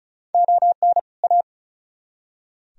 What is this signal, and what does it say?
Morse code, heard as a keyed tone, about 48 wpm, spaced as for 42 wpm, 700 Hz. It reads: ON A
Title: Speed-Racing - Sets of 2 Words 35wpm